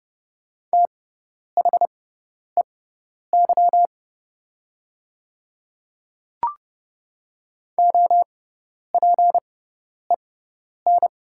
Morse code, heard as a keyed tone, about 30 wpm, spaced as for 10 wpm, 700 Hz. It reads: THEY OPEN
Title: 6x Character Spacing - Top 300 Words - 30wpm